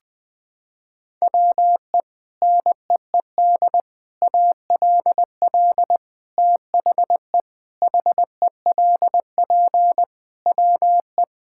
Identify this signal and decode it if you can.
Morse code, heard as a keyed tone, about 20 wpm, 700 Hz. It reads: WE NEED ALL THE HELP WE
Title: Sets of 8 Words - 20wpm